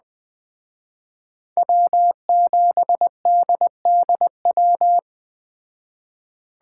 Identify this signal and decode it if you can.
Morse code, heard as a keyed tone, about 20 wpm, 700 Hz. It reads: W7DDW
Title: Speed Racing - USA 1x3 Call Signs 17wpm